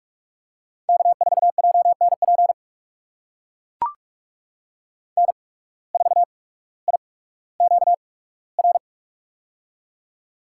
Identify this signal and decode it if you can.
Morse code, heard as a keyed tone, about 44 wpm, spaced as for 19 wpm, 700 Hz. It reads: K4JNP N4IQR
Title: USA Call Signs - 8x Character Spacing - 45wpm